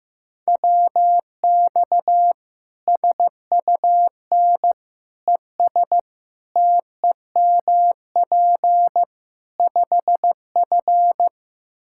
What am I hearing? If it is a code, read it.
Morse code, heard as a keyed tone, about 15 wpm, 700 Hz. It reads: WX SUN ES TEMP 5F
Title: Practice QSO - Exchange 2 - Info - 15wpm